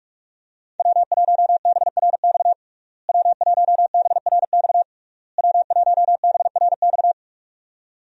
Morse code, heard as a keyed tone, about 46 wpm, 700 Hz. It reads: W1BRX W1BRX W1BRX
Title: USA Call Signs - 5 Characters - Repeated 3x - 45wpm